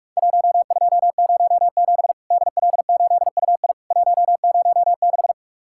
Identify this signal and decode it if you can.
Morse code, heard as a keyed tone, about 44 wpm, 700 Hz. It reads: 1207 DL8UI 106
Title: Worked All Europe DX Contest - QTC Exchange - 45wpm